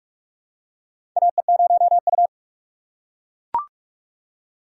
Morse code, heard as a keyed tone, about 44 wpm, 700 Hz. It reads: AE0U
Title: USA Call Signs - 4 Characters - 3x Character Spacing - 45wpm